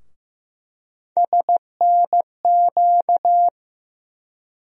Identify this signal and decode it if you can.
Morse code, heard as a keyed tone, about 15 wpm, 700 Hz. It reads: SNQ
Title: USA Call Sign Postfixes - 3 Letters 15wpm